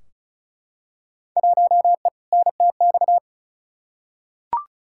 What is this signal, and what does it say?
Morse code, heard as a keyed tone, about 36 wpm, 700 Hz. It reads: 1E NTX
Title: Field Day Exchanges - 35wpm